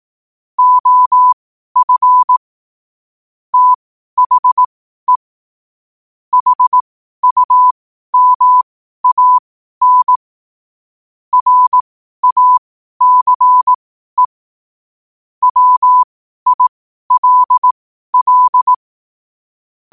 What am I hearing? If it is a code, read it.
Morse code, heard as a keyed tone, about 18 wpm, spaced as for 12 wpm, 1000 Hz. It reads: OF THE HUMAN RACE WILL
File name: qotdfiles / 12wpm